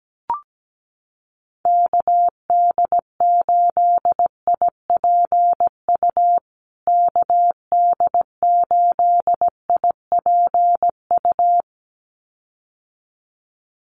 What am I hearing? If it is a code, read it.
Morse code, heard as a keyed tone, about 17 wpm, 700 Hz. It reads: KD8IPU KD8IPU
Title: USA Call Signs Repeated 2x 17wpm